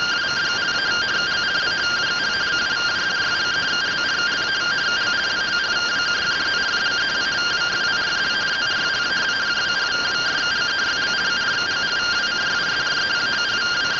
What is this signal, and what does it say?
Two-tone teleprinter signal, about 170 Hz shift.